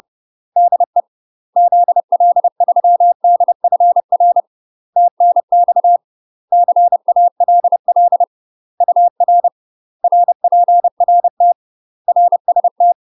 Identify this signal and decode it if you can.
Morse code, heard as a keyed tone, about 30 wpm, 700 Hz. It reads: DE ZL3DFR TNX CALL UR RPRT RST
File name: UK MASTER QSOs TRAINING / 30WPM / UK MASTER QSOs 0019